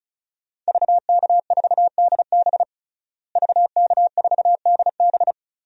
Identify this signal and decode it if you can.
Morse code, heard as a keyed tone, about 36 wpm, 700 Hz. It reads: VK4DB VK4DB
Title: DXCC-100 Call Signs - 5 Characters - Repeated 3x - 35wpm